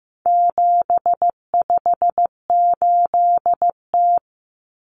Morse code, heard as a keyed tone, about 15 wpm, 700 Hz. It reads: 758T
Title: Common Cut Numbers - 4-Digit 15wpm